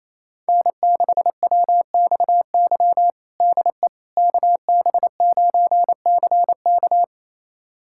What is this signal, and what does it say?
Morse code, heard as a keyed tone, about 28 wpm, 700 Hz. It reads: N6WXY DE KB9CK
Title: USA Callsign DE Callsign - 5 Characters - 28wpm